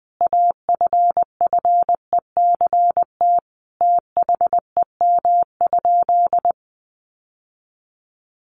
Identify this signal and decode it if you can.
Morse code, heard as a keyed tone, about 20 wpm, 700 Hz. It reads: AFFECT THEM?
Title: Sentences from Top 900 Words 20wpm